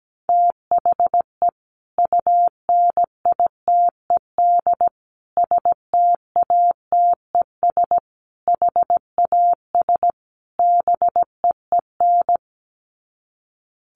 Morse code, heard as a keyed tone, about 17 wpm, 700 Hz. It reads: THE UNITED STATES HAS BEEN
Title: Sets of 5 Words Encore2 17wpm